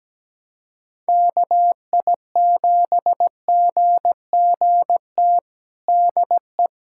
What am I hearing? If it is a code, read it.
Morse code, heard as a keyed tone, about 17 wpm, 700 Hz. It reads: KI7GGT DE
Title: USA Callsign DE Callsign - 6 Characters - 17wpm